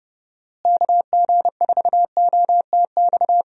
Morse code, heard as a keyed tone, about 32 wpm, 700 Hz. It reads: KG4OTX
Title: USA Call Signs - 6 Characters - Repeated 3x - 30wpm